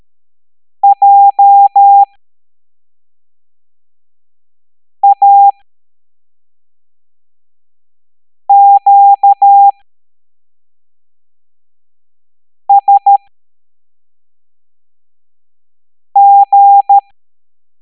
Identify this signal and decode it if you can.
Morse code, heard as a keyed tone, about 13 wpm, spaced as for 3 wpm, 800 Hz. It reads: JAQSG
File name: hamdata / L09-EX04